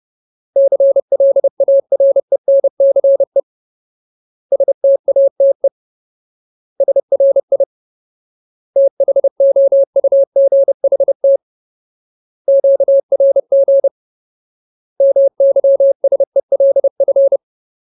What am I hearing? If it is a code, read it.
Morse code, heard as a keyed tone, about 30 wpm, 550 Hz. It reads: CLARENCE STATE SRI THOUGHT QRG MYSELF
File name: Ultimate Word List / 30WPM Ultimate